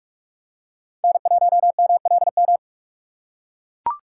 Morse code, heard as a keyed tone, about 44 wpm, 700 Hz. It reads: N1MLM
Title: USA Call Signs - 5 Characters - 6x Character Spacing - 45wpm